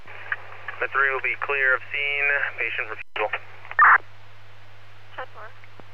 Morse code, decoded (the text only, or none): none